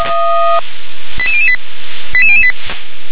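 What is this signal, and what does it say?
Five-tone selective call (ZVEI), heard as quick runs of five tones, tones of about 70 ms. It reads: 88008 88008
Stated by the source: ZVEI five-tone selective call 88017, received by SDR.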